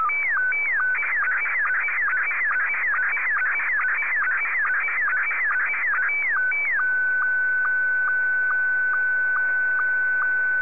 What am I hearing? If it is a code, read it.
SSTV (slow-scan television), Scottie 1.